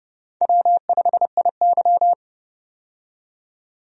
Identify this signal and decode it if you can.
Morse code, heard as a keyed tone, about 30 wpm, 700 Hz. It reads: W5IY
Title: USA Call Signs - 4 Characters - 30wpm